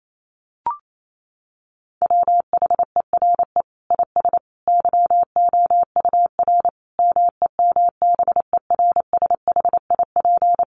Morse code, heard as a keyed tone, about 28 wpm, 700 Hz. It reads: WHERE IS YOUR MEMBERSHIP
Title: Sentences from Top 1400 Words 28wpm